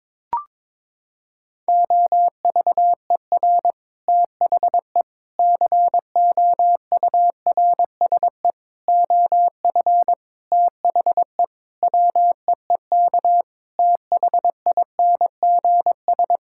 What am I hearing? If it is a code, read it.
Morse code, heard as a keyed tone, about 21 wpm, 700 Hz. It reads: OVER THE COURSE OF THE WEEK THINGS
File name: practice / 22wpm / Sentences with 4-Word Phrases - 22wpm